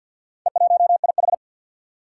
Morse code, heard as a keyed tone, about 54 wpm, 700 Hz. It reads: E1IH
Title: Four Challenging Letter-Number Combinations 50wpm